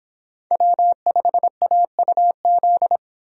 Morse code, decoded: W5AUZ

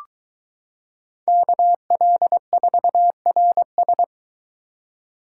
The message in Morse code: KL4RS

